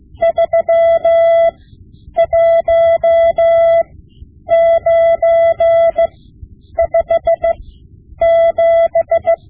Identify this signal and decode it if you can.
Morse code, heard as a keyed tone, about 15 wpm, spaced as for 9 wpm, 670 Hz. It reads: 31957
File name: Morse Lessons / Morse Lessons / Tape-5B